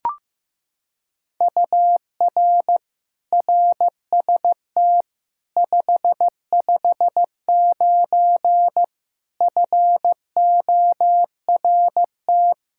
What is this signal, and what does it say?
Morse code, heard as a keyed tone, about 15 wpm, 700 Hz. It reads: UR RST 559 FORT
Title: Practice QSO - Exchange 1 - Beginning - 15wpm